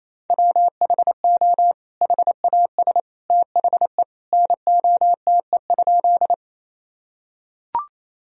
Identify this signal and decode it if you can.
Morse code, heard as a keyed tone, about 27 wpm, 700 Hz. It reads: WHO HAS THE NOTE?
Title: Sentences from Top 1000 Words - Review Part 2 - 28wpm